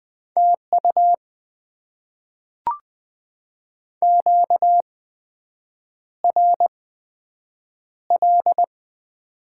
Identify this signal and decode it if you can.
Morse code, heard as a keyed tone, about 21 wpm, spaced as for 7 wpm, 700 Hz. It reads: TU QRL
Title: Top 20 QSO Elements - 8x Character Spacing - 20wpm